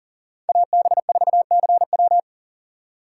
Morse code, heard as a keyed tone, about 39 wpm, 700 Hz. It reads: AB4CW
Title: USA Call Signs - 2x Character Spacing - 40wpm